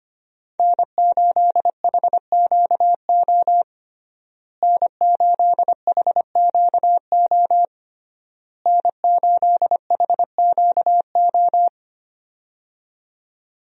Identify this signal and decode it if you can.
Morse code, heard as a keyed tone, about 25 wpm, 700 Hz. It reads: N8HQO N8HQO N8HQO
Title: USA Call Signs - 5 Characters - Repeated 3x - 25wpm